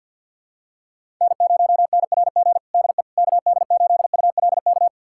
Morse code, heard as a keyed tone, about 52 wpm, 700 Hz. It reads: N0NRG DE KD8ULK
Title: USA Callsign DE Callsign 50wpm